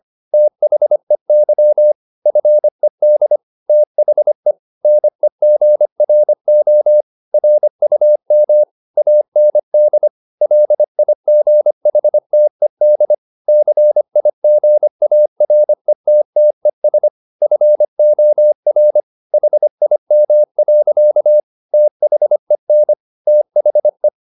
Morse code, heard as a keyed tone, about 25 wpm, 600 Hz. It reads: THEY FED THE NEGRO RUM AND LIGHTED CIGARETTES FOR HIM. THEN THE